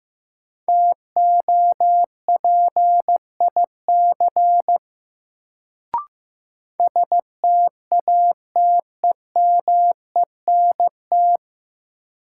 Morse code, heard as a keyed tone, about 15 wpm, 700 Hz. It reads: TOPIC STATEMENT
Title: Sentences from Top 1200 Words - New Word Drill - 15wpm